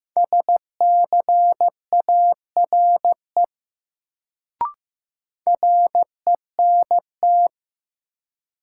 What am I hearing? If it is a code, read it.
Morse code, heard as a keyed tone, about 15 wpm, 700 Hz. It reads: SCARE RENT
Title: Sentences from Top 1000 Words - Review Part 5 - New Word Drill - 15wpm